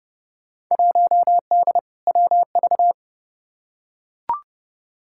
Morse code, heard as a keyed tone, about 30 wpm, 700 Hz. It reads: 1D WV
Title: Field Day Exchanges - 30wpm